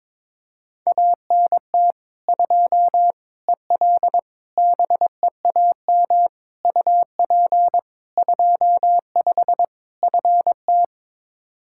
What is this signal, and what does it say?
Morse code, heard as a keyed tone, about 22 wpm, 700 Hz. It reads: ANT 2 EL BEAM UP 25 FT
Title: Practice QSO - Exchange 2 - Info - 22wpm